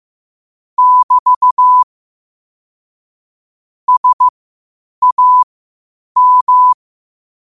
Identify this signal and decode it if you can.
Morse code, heard as a keyed tone, about 15 wpm, spaced as for 8 wpm, 1000 Hz. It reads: <BT> SAM
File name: qotdfiles / 8wpm